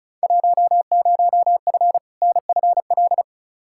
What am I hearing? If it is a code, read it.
Morse code, heard as a keyed tone, about 36 wpm, 700 Hz. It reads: 10F NFL
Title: Field Day Exchanges Encore - 35wpm